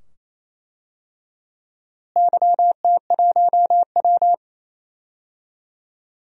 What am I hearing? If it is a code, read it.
Morse code, heard as a keyed tone, about 28 wpm, 700 Hz. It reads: YT1W
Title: DXCC-100 Call Signs - 4 Characters - Repeated 2x - 28wpm